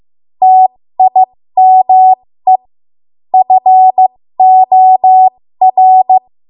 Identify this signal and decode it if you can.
Morse code, heard as a keyed tone, about 15 wpm, spaced as for 13 wpm, 750 Hz. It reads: TIME FOR
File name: Archive / 13 WPM / 13WPM